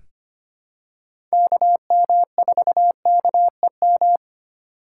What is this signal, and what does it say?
Morse code, heard as a keyed tone, about 25 wpm, 700 Hz. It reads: KM4KEM
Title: USA Call Signs - 8x Character Spacing - 25wpm